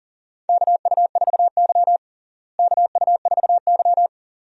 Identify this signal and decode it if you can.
Morse code, heard as a keyed tone, about 40 wpm, 700 Hz. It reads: KU4Y KU4Y
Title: USA Call Signs - 4 Characters - Repeated 2x - 40wpm